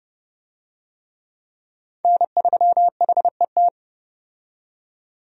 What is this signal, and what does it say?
Morse code, heard as a keyed tone, about 30 wpm, 700 Hz. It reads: N3HET
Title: USA Call Signs - 5 Characters - 30wpm